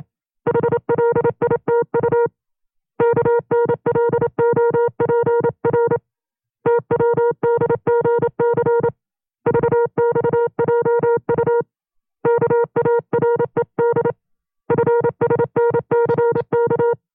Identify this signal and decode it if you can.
Morse code, heard as a keyed tone, about 27 wpm, 470 Hz. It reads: HLITU KNLOPR TWDGC VXJU KARED FSNCK
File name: FreeFiles / CW-practice / 25WPM-side1